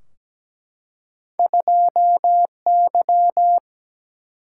Morse code, heard as a keyed tone, about 17 wpm, 700 Hz. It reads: 2Y
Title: Single-Letter Single-Number 17wpm